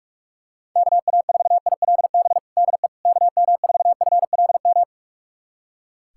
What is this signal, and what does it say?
Morse code, heard as a keyed tone, about 41 wpm, 700 Hz. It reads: KA4ILB DE KM4FLM
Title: USA Callsign DE Callsign 45wpm